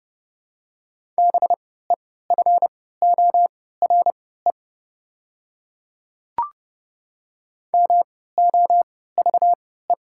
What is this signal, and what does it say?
Morse code, heard as a keyed tone, about 30 wpm, spaced as for 17 wpm, 700 Hz. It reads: BEFORE MOVE
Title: 3x Character Spacing - Top 200 Words - 30wpm